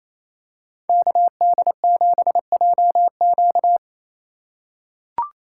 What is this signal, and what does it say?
Morse code, heard as a keyed tone, about 28 wpm, 700 Hz. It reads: KD7JQ
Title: USA Call Signs - 5 Characters - 4x Character Spacing - 28wpm